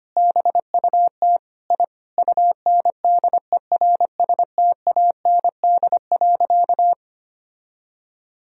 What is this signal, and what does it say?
Morse code, heard as a keyed tone, about 25 wpm, 700 Hz. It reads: BUT I UNDERSTAND.
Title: Sentences from Top 1000 Words - Review Part 4 - 25wpm